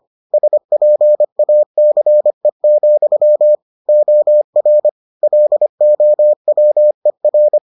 Morse code, heard as a keyed tone, about 25 wpm, 600 Hz. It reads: SPACE, OR LOWER